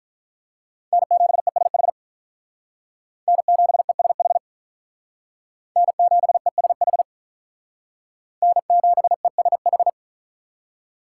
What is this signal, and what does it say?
Morse code, heard as a keyed tone, about 46 wpm, 700 Hz. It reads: N7ESH N7ESH N7ESH N7ESH